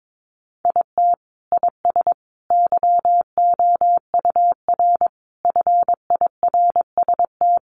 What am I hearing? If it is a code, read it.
Morse code, heard as a keyed tone, about 22 wpm, 700 Hz. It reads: IT IS YOUR FIRST
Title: Sentences from Top 1100 Words 22wpm